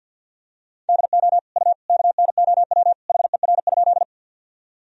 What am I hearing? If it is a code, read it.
Morse code, heard as a keyed tone, about 47 wpm, 700 Hz. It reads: DO U KNOW HER?
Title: Sentences from Top 100 Words and Abbreviations 50wpm